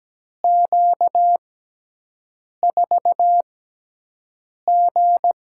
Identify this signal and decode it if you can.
Morse code, heard as a keyed tone, about 17 wpm, spaced as for 6 wpm, 700 Hz. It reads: Q4G
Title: USA Call Signs - 4 Characters - 6x Character Spacing - 17wpm